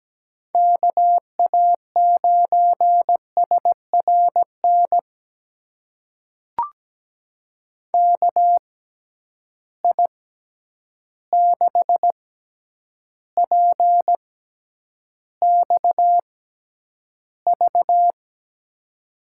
Morse code, heard as a keyed tone, about 17 wpm, spaced as for 9 wpm, 700 Hz. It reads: KA9SRN KI6PXV
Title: USA Call Signs - 6 Characters - 6x Character Spacing - 17wpm